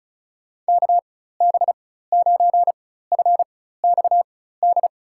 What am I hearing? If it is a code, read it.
Morse code, heard as a keyed tone, about 36 wpm, spaced as for 16 wpm, 700 Hz. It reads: KB9FXD
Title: USA Call Signs - 6 Characters - 4x Character Spacing - 35wpm